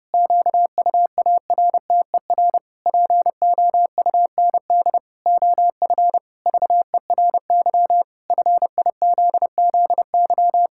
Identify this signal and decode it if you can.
Morse code, heard as a keyed tone, about 30 wpm, 700 Hz. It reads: QUARTER POUND OF VERY FIZZY
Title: Pangram Challenge v2 - 30wpm